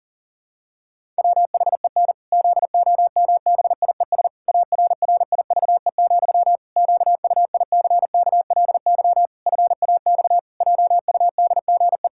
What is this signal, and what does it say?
Morse code, heard as a keyed tone, about 40 wpm, 700 Hz. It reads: WHEN ZOMBIES ARRIVE, QUICKLY FAX JUDGE
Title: Pangram Challenge v2 - 40wpm